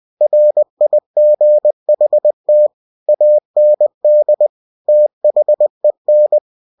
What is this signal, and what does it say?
Morse code, heard as a keyed tone, about 20 wpm, 600 Hz. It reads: RIGHT AND THEN
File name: cw-books / winnie-the-pooh / winnie-20wpm-page-0047